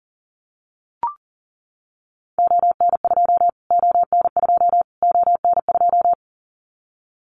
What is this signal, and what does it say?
Morse code, heard as a keyed tone, about 40 wpm, 700 Hz. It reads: ON2 ON2 ON2
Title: Callsign Prefixes for Top 25 DXCC - Repeated 3x - 40wpm